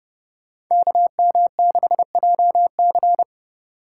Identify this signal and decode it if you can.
Morse code, heard as a keyed tone, about 30 wpm, 700 Hz. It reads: KM6JC
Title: Code Groups - 5 Letter:Numbers - 30wpm